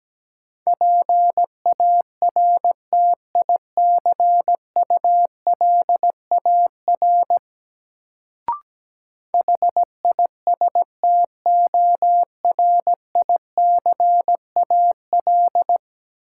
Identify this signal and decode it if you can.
Morse code, heard as a keyed tone, about 17 wpm, 700 Hz. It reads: PARTICULAR HISTORICAL
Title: Sentences from Top 600 Words - New Word Drill - 17wpm